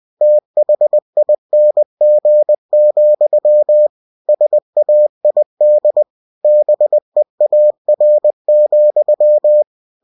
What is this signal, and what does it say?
Morse code, heard as a keyed tone, about 20 wpm, 600 Hz. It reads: THING, SAID BEAR,
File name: cw-books / winnie-the-pooh / winnie-20wpm-page-0022